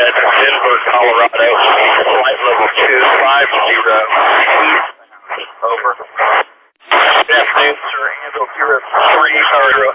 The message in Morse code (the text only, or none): none